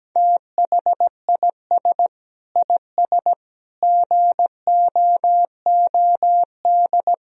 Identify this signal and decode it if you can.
Morse code, heard as a keyed tone, about 17 wpm, 700 Hz. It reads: THIS IS GOOD
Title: Sentences from Top 1000 Words - Review Part 4 - 17wpm